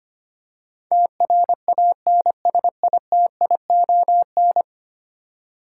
Morse code, heard as a keyed tone, about 25 wpm, 700 Hz. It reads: TRANSITION
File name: Ten Letter Words - 25wpm